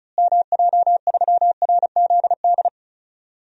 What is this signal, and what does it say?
Morse code, heard as a keyed tone, about 36 wpm, 700 Hz. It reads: MJ3RZD
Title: DXCC-100 Call Signs - 6 Characters - Repeated 3x - 35wpm